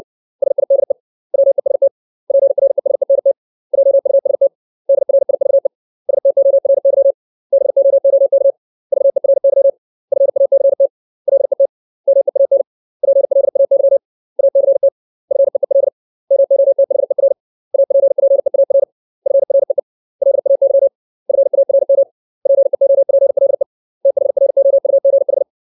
Morse code, heard as a keyed tone, about 60 wpm, 550 Hz. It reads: SIDE WEST PRESENT JUST DRIVE STORY BOOK VERY FACT LET MEAN PLAY ACT FEEL MOTHER APPEAR FREE LAY FARM PEOPLE THROUGH
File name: 632 words / 632 Words 60WPM